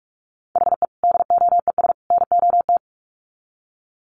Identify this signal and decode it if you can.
Morse code, heard as a keyed tone, about 44 wpm, 700 Hz. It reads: HE DOES NOT
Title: Sets of 3 Words 45wpm